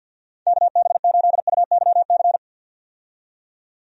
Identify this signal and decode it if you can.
Morse code, heard as a keyed tone, about 47 wpm, 700 Hz. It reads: KB8UYC
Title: USA Call Signs - 6 Characters - 50wpm